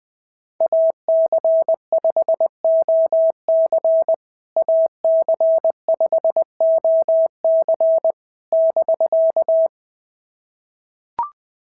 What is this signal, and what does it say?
Morse code, heard as a keyed tone, about 20 wpm, 650 Hz. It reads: AC5OC AC5OC <BK>
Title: POTA QSO Practice - 20wpm